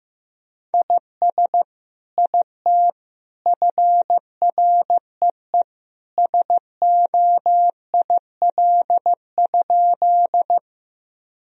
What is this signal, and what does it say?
Morse code, heard as a keyed tone, about 15 wpm, 700 Hz. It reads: IS IT FREE SOIL?
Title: Sentences from Top 1000 Words - Review Part 4 - 15wpm